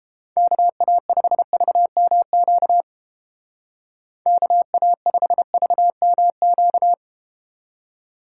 Morse code, heard as a keyed tone, about 31 wpm, 700 Hz. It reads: KA5VMQ KA5VMQ